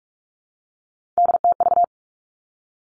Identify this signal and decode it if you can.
Morse code, heard as a keyed tone, about 46 wpm, 700 Hz. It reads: DTV
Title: Three Challenging Letter-Number Combinations 45wpm